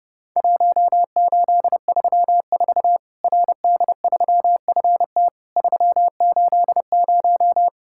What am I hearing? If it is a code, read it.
Morse code, heard as a keyed tone, about 30 wpm, 700 Hz. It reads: 1834 RD3FT 380
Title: Worked All Europe DX Contest - QTC Exchange - 30wpm